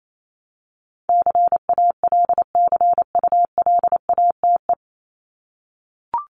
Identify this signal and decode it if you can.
Morse code, heard as a keyed tone, about 28 wpm, 700 Hz. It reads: CALCULATE
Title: Top 1751-2000 Words 28wpm